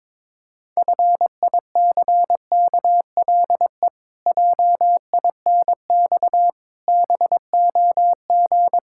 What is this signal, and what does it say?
Morse code, heard as a keyed tone, about 22 wpm, 700 Hz. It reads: FICKLE JINX BOG
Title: Pangram Challenge v2 - 22wpm